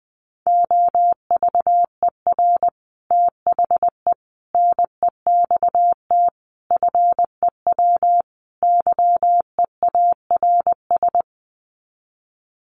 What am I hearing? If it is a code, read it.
Morse code, heard as a keyed tone, about 20 wpm, 700 Hz. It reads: OVER THE NEXT FEW YEARS
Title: Sets of 5 Words Encore 20wpm